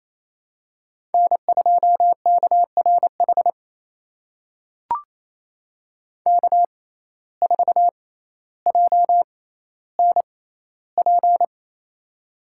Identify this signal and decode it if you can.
Morse code, heard as a keyed tone, about 28 wpm, spaced as for 14 wpm, 700 Hz. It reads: N2KRH K4JNP
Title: USA Call Signs - 5 Characters - 6x Character Spacing - 28wpm